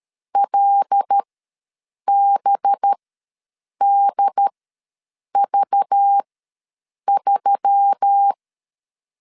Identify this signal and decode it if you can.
Morse code, heard as a keyed tone, about 13 wpm, spaced as for 7 wpm, 800 Hz. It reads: LBDV3
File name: LEZIONI / 5 LBVC / 5 LBV ws 12wpm 3015